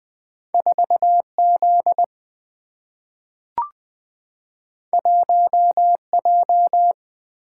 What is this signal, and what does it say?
Morse code, heard as a keyed tone, about 20 wpm, 700 Hz. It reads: 4Z 1J
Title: Two Letter-Number Combinations - 20wpm